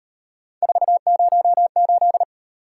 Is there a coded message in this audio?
Yes: Morse code from 0.6 to 2.2 s, heard as a keyed tone, about 38 wpm, 700 Hz. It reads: 408